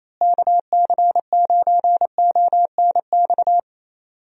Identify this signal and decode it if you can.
Morse code, heard as a keyed tone, about 28 wpm, 700 Hz. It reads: KC9ONX
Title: USA Call Signs - 6 Characters - Repeated 3x - 28wpm